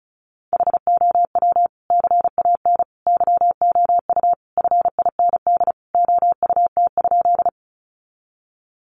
Morse code, heard as a keyed tone, about 34 wpm, 700 Hz. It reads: HOW CAN YOU FIND OUT?
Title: Sentences from Top 200 Words 35wpm